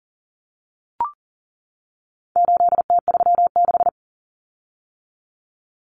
Morse code, heard as a keyed tone, about 40 wpm, 700 Hz. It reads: 8T36
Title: Common Cut Numbers - 4-Digit 40wpm